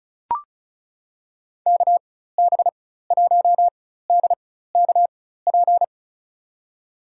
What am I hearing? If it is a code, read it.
Morse code, heard as a keyed tone, about 36 wpm, spaced as for 16 wpm, 700 Hz. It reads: KB1DKP
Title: USA Call Signs - 6 Characters - 4x Character Spacing - 35wpm